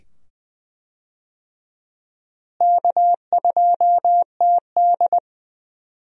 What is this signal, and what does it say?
Morse code, heard as a keyed tone, about 20 wpm, 700 Hz. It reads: K2TD
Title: USA Call Signs - 4 Characters - Repeated 2x - 20wpm